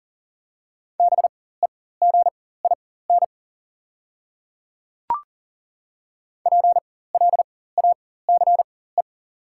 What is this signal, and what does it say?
Morse code, heard as a keyed tone, about 40 wpm, spaced as for 19 wpm, 690 Hz. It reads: BEGIN PLACE
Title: 4x Character Spacing - Top 500 Words - 40wpm